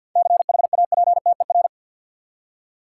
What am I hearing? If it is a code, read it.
Morse code, heard as a keyed tone, about 50 wpm, 700 Hz. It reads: CHAPTER
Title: Seven Letter Words - 50wpm